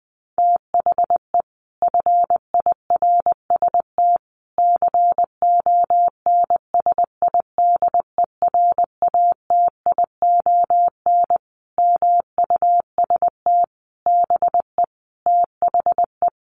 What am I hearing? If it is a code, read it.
Morse code, heard as a keyed tone, about 19 wpm, 700 Hz. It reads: THE FIRST CONSIDERATION MUST BE THE
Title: Sentences from Top 1500 Words 20wpm